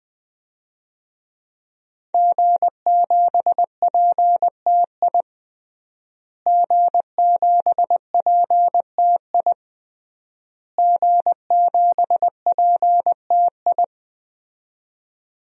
Morse code, heard as a keyed tone, about 20 wpm, 700 Hz. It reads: G7PTI G7PTI G7PTI
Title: DXCC-100 Call Signs - 5 Characters - Repeated 3x - 20wpm